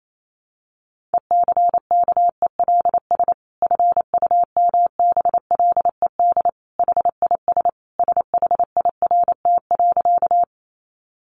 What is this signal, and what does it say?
Morse code, heard as a keyed tone, about 28 wpm, 700 Hz. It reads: ECKELS FUMBLED HIS SHIRT.
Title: A Sound Of Thunder 28wpm